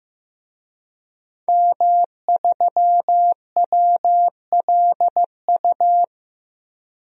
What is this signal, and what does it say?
Morse code, heard as a keyed tone, about 15 wpm, 700 Hz. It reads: M3WLU
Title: DXCC-100 Call Signs - 5 Characters - Repeated 3x - 15wpm